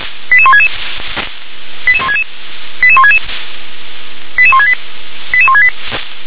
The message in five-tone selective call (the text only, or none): 88166 88166 88166 88168 88168